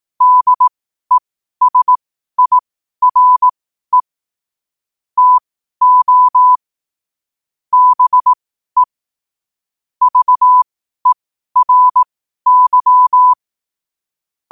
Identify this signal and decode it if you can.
Morse code, heard as a keyed tone, about 18 wpm, spaced as for 12 wpm, 1000 Hz. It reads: DESIRE TO BE VERY
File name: qotdfiles / 12wpm